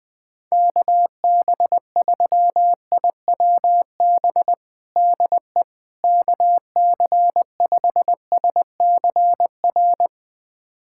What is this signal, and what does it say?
Morse code, heard as a keyed tone, about 20 wpm, 700 Hz. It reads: KB3IWB DE KC5SCR
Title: USA Callsign DE Callsign - 6 Characters - 20wpm